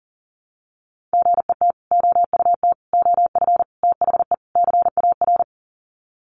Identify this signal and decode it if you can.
Morse code, heard as a keyed tone, about 40 wpm, 700 Hz. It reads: GET OUT OF THE CAR
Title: Sets of 5 Words Encore2 40wpm